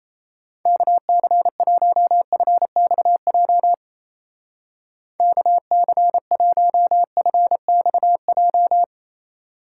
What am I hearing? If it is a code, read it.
Morse code, heard as a keyed tone, about 36 wpm, 700 Hz. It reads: KC1FXJ KC1FXJ